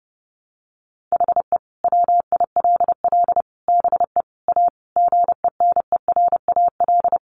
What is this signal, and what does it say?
Morse code, heard as a keyed tone, about 30 wpm, 700 Hz. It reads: HE WILL BE A GENERAL
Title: Sentences from Top 1200 Words 30wpm